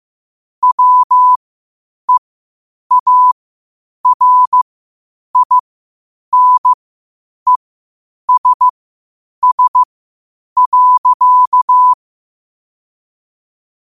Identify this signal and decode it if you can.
Morse code, heard as a keyed tone, about 15 wpm, spaced as for 8 wpm, 1000 Hz. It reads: WEARINESS.
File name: qotdfiles / 8wpm